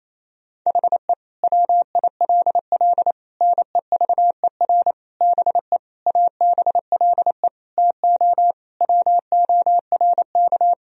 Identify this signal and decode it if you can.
Morse code, heard as a keyed tone, about 28 wpm, 700 Hz. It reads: HE WILL NEVER BE ABLE TO WORK
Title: Sentences from Top 1000 Words - Review Part 2 - 28wpm